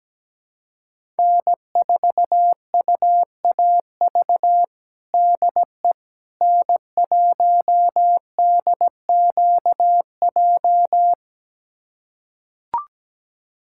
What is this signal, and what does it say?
Morse code, heard as a keyed tone, about 17 wpm, 700 Hz. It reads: N4UAV DE N1DQJ
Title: USA Callsign DE Callsign - 5 Characters - 17wpm